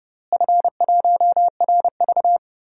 Morse code, heard as a keyed tone, about 30 wpm, 700 Hz. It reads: F1RV
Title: DXCC-100 Call Signs - 4 Characters - Repeated 3x - 30wpm